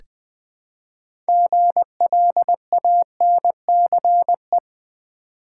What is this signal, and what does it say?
Morse code, heard as a keyed tone, about 20 wpm, 700 Hz. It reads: GLANCE
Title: Build A Word - 2 Letters At A Time - 6-Letter Words - 20wpm